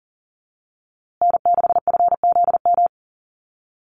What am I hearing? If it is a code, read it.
Morse code, heard as a keyed tone, about 39 wpm, 700 Hz. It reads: N6FZM
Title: USA Call Signs Repeated 2x 40wpm